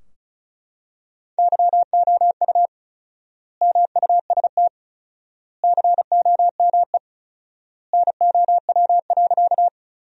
Morse code, heard as a keyed tone, about 36 wpm, 700 Hz. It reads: YOU MUST COME NOW.
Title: 4x Word-Spacing - Sentences from Top 200 Words - 35wpm